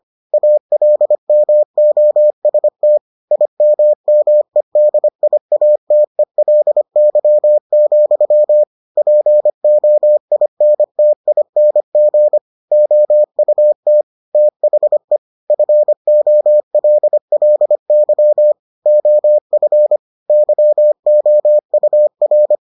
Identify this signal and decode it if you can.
Morse code, heard as a keyed tone, about 25 wpm, 600 Hz. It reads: ALMOST IMMEDIATELY, POINTING OUT THE FOLLY OF YOUR